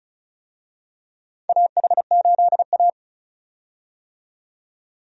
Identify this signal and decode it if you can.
Morse code, heard as a keyed tone, about 36 wpm, 700 Hz. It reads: AH8A
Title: USA Call Signs - 4 Characters - 35wpm